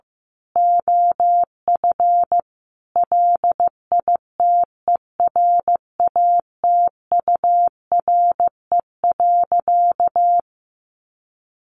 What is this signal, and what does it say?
Morse code, heard as a keyed tone, about 14 wpm, 700 Hz. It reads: OF LITERATURE.
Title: Sentences from Top 900 Words 15wpm